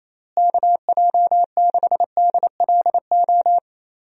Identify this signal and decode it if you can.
Morse code, heard as a keyed tone, about 28 wpm, 700 Hz. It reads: KJ6DLO
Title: USA Call Signs - 6 Characters - Repeated 3x - 28wpm